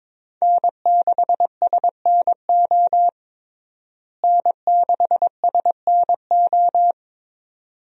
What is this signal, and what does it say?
Morse code, heard as a keyed tone, about 22 wpm, 700 Hz. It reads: N6SNO N6SNO
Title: USA Call Signs - 5 Characters - Repeated 2x - 22wpm